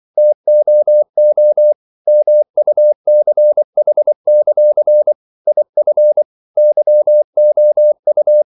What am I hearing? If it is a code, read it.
Morse code, heard as a keyed tone, about 24 wpm, 600 Hz. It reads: TOO MUCH; IF YOU
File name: CW / Practice / If24